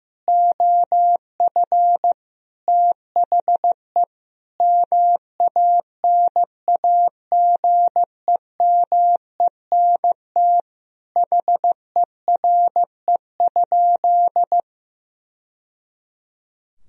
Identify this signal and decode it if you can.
Morse code, heard as a keyed tone, about 15 wpm, 700 Hz. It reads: OF THE MANAGEMENT HERE?
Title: Sentences from Top 1000 Words - Review Part 4 - 15wpm